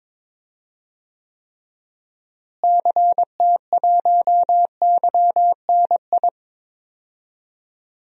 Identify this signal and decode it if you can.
Morse code, heard as a keyed tone, about 22 wpm, 700 Hz. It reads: CT1YNI
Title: DXCC-100 Call Signs - 6 Characters - Repeated 2x - 22wpm